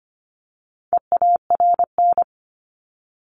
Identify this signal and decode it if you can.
Morse code, heard as a keyed tone, about 25 wpm, 700 Hz. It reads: EARN
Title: Top 1251-1500 Words 25wpm